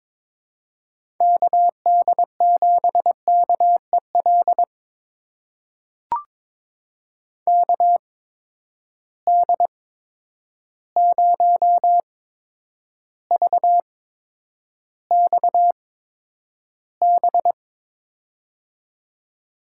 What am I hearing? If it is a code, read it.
Morse code, heard as a keyed tone, about 22 wpm, spaced as for 9 wpm, 700 Hz. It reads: KD7KEL KD0VXB
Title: USA Call Signs - 6 Characters - 8x Character Spacing - 22wpm